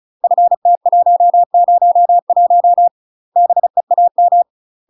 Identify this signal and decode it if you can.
Morse code, heard as a keyed tone, about 35 wpm, 700 Hz. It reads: FT101 BEAM
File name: UK MASTER QSOs TRAINING/35WPM